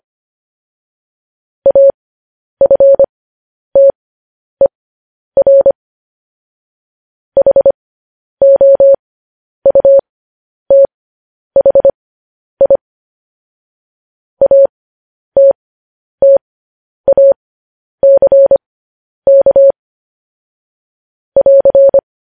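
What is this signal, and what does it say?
Morse code, heard as a keyed tone, about 25 wpm, spaced as for 10 wpm, 550 Hz. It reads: AFTER HOUTHI ATTACK <AR>
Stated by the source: Morse code news headlines sent at 10.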